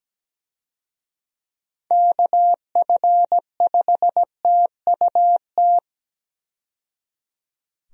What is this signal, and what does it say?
Morse code, heard as a keyed tone, about 17 wpm, 700 Hz. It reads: KF5TUT
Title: USA Call Signs - 6 Characters - 17wpm